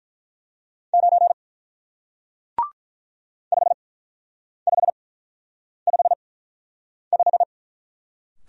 Morse code, heard as a keyed tone, about 48 wpm, spaced as for 9 wpm, 700 Hz. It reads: 9 5555